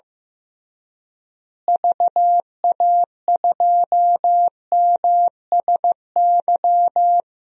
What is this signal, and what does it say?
Morse code, heard as a keyed tone, about 16 wpm, 700 Hz. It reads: VA2MSY
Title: DXCC-100 Call Signs - 6 Characters - Repeated 3x - 15wpm